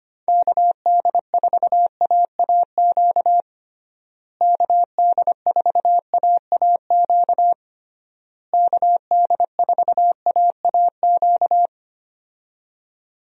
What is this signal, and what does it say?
Morse code, heard as a keyed tone, about 25 wpm, 700 Hz. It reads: KD4AAQ KD4AAQ KD4AAQ
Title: USA Call Signs - 6 Characters - Repeated 3x - 25wpm